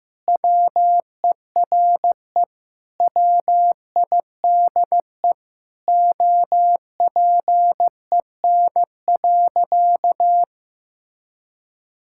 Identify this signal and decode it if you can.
Morse code, heard as a keyed tone, about 14 wpm, 700 Hz. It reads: WERE WIDE OPEN.
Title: Sentences from Top 1000 Words - Review Part 6 - 15wpm